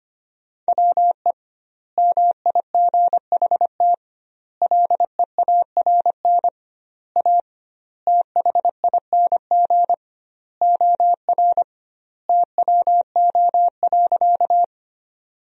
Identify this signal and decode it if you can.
Morse code, heard as a keyed tone, about 25 wpm, 700 Hz. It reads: WE MIGHT LEARN A THING OR TWO.
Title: 2x Word-Spacing - Sentences from Top 300 Words - 25wpm